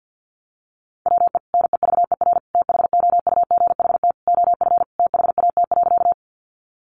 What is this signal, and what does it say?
Morse code, heard as a keyed tone, about 48 wpm, 700 Hz. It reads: WE NEVER THOUGHT OF THAT.
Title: Sentences from Top 300 Words 50wpm